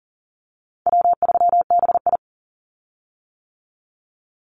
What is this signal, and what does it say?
Morse code, heard as a keyed tone, about 40 wpm, 700 Hz. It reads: W3BI
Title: Easy Call Signs - 40wpm